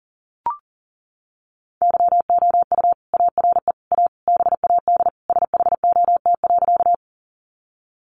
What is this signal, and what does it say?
Morse code, heard as a keyed tone, about 40 wpm, 700 Hz. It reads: YOU ARE A BAD SHOT.
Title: Sentences from Top 1100 Words 40wpm